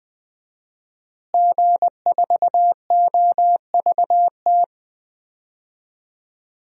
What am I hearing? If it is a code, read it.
Morse code, heard as a keyed tone, about 20 wpm, 700 Hz. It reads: G4OVT
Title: DXCC-100 Call Signs - 5 Characters - Repeated 3x - 20wpm